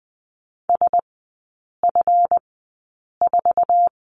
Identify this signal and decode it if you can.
Morse code, heard as a keyed tone, about 20 wpm, spaced as for 8 wpm, 690 Hz. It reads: SF4
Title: 212 Lesson 34e - Number 8 - Three Characters 20wpm